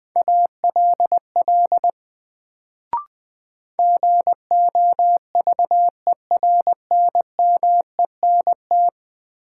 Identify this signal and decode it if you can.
Morse code, heard as a keyed tone, about 20 wpm, 700 Hz. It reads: ALL GOVERNMENT
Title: Sets of 2 Words - New Word Drill - 20wpm